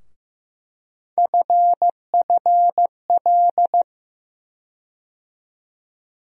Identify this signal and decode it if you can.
Morse code, heard as a keyed tone, about 15 wpm, 700 Hz. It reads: FFL
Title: USA Call Sign Postfixes - 3 Letters 15wpm